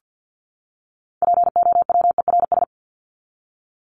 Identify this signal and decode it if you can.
Morse code, heard as a keyed tone, about 50 wpm, 700 Hz. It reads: POWERS
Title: ES:S Suffix - Top 100 Words with Base Words 50wpm